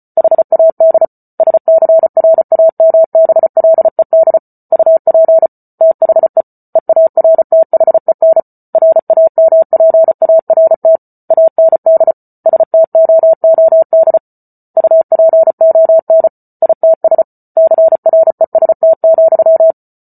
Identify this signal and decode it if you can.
Morse code, heard as a keyed tone, about 34 wpm, 650 Hz. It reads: HAD SCRAMBLED UP THE EARTHEN RAMPART AND STOOD UPON ITS CREST,